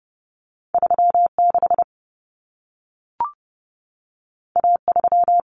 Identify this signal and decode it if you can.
Morse code, heard as a keyed tone, about 30 wpm, 700 Hz. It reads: 36 A3
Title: Common Cut Numbers - 2-Digit 30wpm